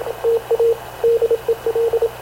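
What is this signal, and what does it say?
Morse code, heard as a keyed tone, about 26 wpm, 460 Hz, with radio noise. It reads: TA DEL
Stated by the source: Havana Radio / CLA sending Spanish press (PX) in Morse, recorded by Radio Officer Paul Dumesnil onboard CNAV QUEST/CZDO.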